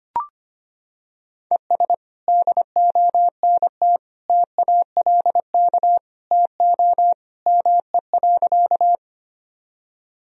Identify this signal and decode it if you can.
Morse code, heard as a keyed tone, about 25 wpm, 700 Hz. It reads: ES DONT TALK TO ME.
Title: Sentences from Top 400 Words and Abbreviations 25wpm